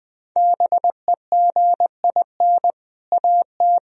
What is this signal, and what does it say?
Morse code, heard as a keyed tone, about 20 wpm, 700 Hz. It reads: BEGIN AT
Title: Sentences from Top 1000 Words - Review Part 1 - 20wpm